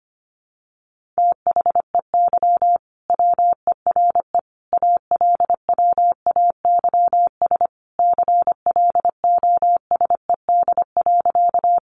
Morse code, heard as a keyed tone, about 25 wpm, 700 Hz. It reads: THEY WERE ALWAYS CLOSED.
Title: Sentences from Top 1400 Words 25wpm